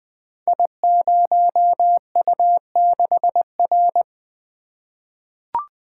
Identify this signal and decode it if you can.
Morse code, heard as a keyed tone, about 20 wpm, 700 Hz. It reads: I0U6R
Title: Code Groups - 5 Letter:Numbers - 20wpm